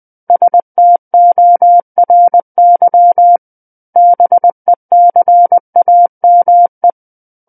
Morse code, heard as a keyed tone, about 20 wpm, 700 Hz. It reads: STORY BECAME
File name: 20 WPM